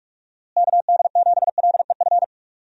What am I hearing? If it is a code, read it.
Morse code, heard as a keyed tone, about 46 wpm, 700 Hz. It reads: KD7LEF
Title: USA Call Signs - 6 Characters - Repeated 2x - 45wpm